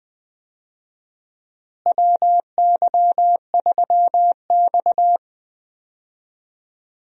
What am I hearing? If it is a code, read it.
Morse code, heard as a keyed tone, about 21 wpm, 700 Hz. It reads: WY3X
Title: USA Call Signs - 4 Characters - Repeated 2x - 20wpm